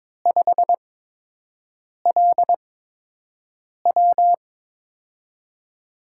Morse code, heard as a keyed tone, about 22 wpm, 700 Hz. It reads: 5 L W
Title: USA Call Signs - 5 Characters - 8x Character Spacing - 22wpm